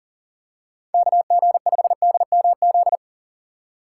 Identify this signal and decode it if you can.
Morse code, heard as a keyed tone, about 40 wpm, 700 Hz. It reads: KG5DMZ
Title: USA Call Signs - 6 Characters - 2x Character Spacing - 40wpm